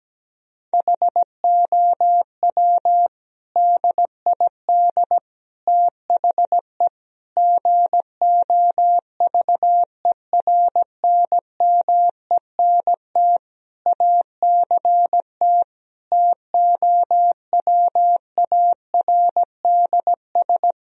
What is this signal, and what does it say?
Morse code, heard as a keyed tone, about 17 wpm, 700 Hz. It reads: HOW DID THE GOVERNMENT ACT TOWARDS
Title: Sentences from Top 1000 Words - Review Part 4 - 17wpm